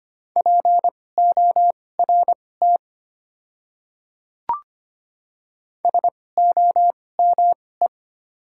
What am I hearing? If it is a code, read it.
Morse code, heard as a keyed tone, about 25 wpm, spaced as for 18 wpm, 690 Hz. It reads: PORT SOME
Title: 2x Character Spacing - Top 200 Words - 25wpm